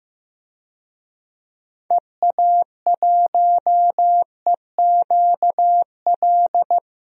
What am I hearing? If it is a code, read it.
Morse code, heard as a keyed tone, about 15 wpm, 700 Hz. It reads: EA1EQL
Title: DXCC-100 Call Signs - 6 Characters - Repeated 2x - 15wpm